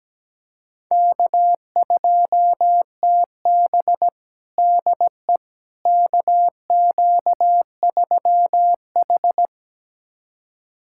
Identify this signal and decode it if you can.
Morse code, heard as a keyed tone, about 17 wpm, 700 Hz. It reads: K2TB DE KQ3H
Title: USA Callsign DE Callsign - 4 Characters - 17wpm